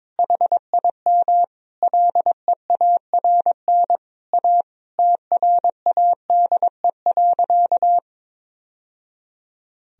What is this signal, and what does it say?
Morse code, heard as a keyed tone, about 22 wpm, 700 Hz. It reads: HIM LEARN A TRADE.
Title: Sentences from Top 1000 Words - Review Part 4 - 22wpm